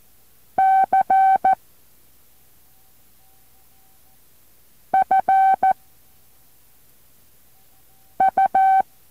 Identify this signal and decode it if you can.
Morse code, heard as a keyed tone, about 14 wpm, 770 Hz. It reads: C F U